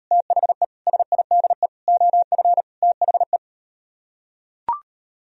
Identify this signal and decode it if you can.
Morse code, heard as a keyed tone, about 38 wpm, 700 Hz. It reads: THE SIDE OF THE